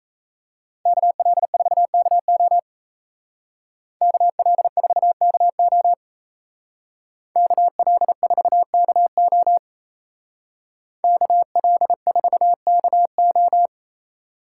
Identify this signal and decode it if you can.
Morse code, heard as a keyed tone, about 34 wpm, 700 Hz. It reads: KL4KO KL4KO KL4KO KL4KO
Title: Speed Racing - USA 2x2 Call Signs 28wpm